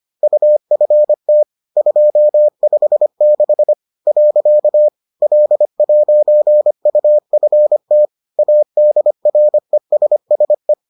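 Morse code, heard as a keyed tone, about 25 wpm, 600 Hz. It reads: UFT 256 . L'UFT ADRESSE
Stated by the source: Bulletin sent in CW.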